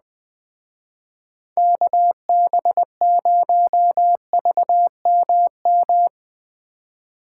Morse code, heard as a keyed tone, about 21 wpm, 700 Hz. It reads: KB0VMM